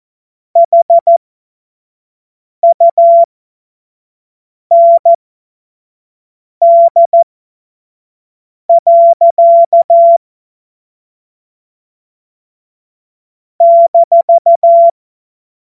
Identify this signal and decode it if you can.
Morse code, heard as a keyed tone, about 14 wpm, spaced as for 5 wpm, 680 Hz. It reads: HUND. -
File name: Morsekurs / Witze / Witz 05 März 2024 25 F70